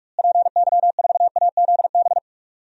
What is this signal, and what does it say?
Morse code, heard as a keyed tone, about 41 wpm, 700 Hz. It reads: PY4AZB
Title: DXCC-100 Call Signs - 6 Characters - Repeated 3x - 45wpm